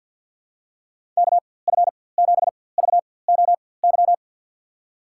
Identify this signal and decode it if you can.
Morse code, heard as a keyed tone, about 53 wpm, spaced as for 24 wpm, 700 Hz. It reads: KF7VOY
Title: USA Call Signs - 6 Characters - 4x Character Spacing - 50wpm